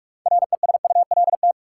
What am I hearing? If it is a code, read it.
Morse code, heard as a keyed tone, about 43 wpm, 700 Hz. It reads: RESULT
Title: Sentences from Top 700 Words - New Word Drill - 45wpm